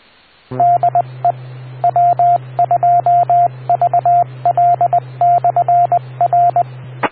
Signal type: Morse code